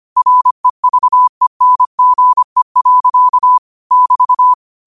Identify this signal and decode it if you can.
Morse code, heard as a keyed tone, about 25 wpm, 1000 Hz. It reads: REVENGE. <BT>
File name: qotdfiles / 25wpm